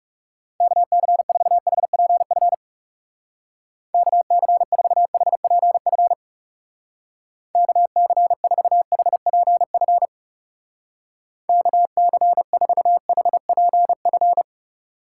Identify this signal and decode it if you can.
Morse code, heard as a keyed tone, about 40 wpm, 700 Hz. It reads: KC4HPF KC4HPF KC4HPF KC4HPF